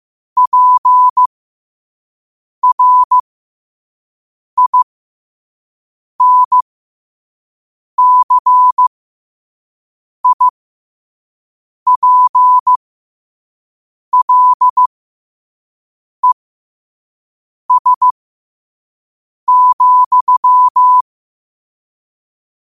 Morse code, heard as a keyed tone, about 15 wpm, spaced as for 5 wpm, 1000 Hz. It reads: PRINCIPLES,
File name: qotdfiles / 5wpm